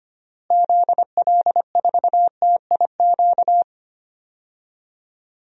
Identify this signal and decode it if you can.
Morse code, heard as a keyed tone, about 25 wpm, 690 Hz. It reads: ZL4TIQ
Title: DXCC-100 Call Signs - 6 Characters - Repeated 3x - 25wpm